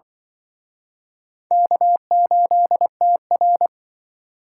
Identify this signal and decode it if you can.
Morse code, heard as a keyed tone, about 25 wpm, 700 Hz. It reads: K8TR